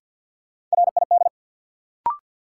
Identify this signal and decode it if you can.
Morse code, heard as a keyed tone, about 50 wpm, 700 Hz. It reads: RID
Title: Sets of 4 Words - New Word Drill - 50wpm